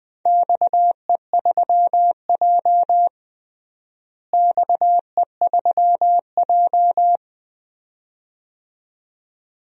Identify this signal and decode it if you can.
Morse code, heard as a keyed tone, about 20 wpm, 700 Hz. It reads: XE3J XE3J
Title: DXCC-100 Call Signs - 4 Characters - Repeated 2x - 20wpm